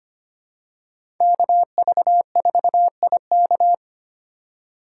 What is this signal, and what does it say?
Morse code, heard as a keyed tone, about 25 wpm, 700 Hz. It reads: KV4IK
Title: USA Call Signs - 6x Character Spacing - 25wpm